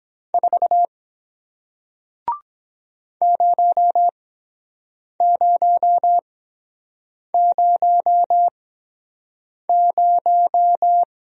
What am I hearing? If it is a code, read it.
Morse code, heard as a keyed tone, about 23 wpm, spaced as for 7 wpm, 700 Hz. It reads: 4 0000